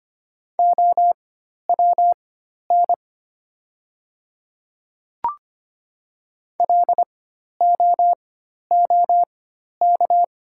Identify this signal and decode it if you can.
Morse code, heard as a keyed tone, about 25 wpm, spaced as for 12 wpm, 700 Hz. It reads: OWN LOOK
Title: 4x Character Spacing - Top 300 Words - 25wpm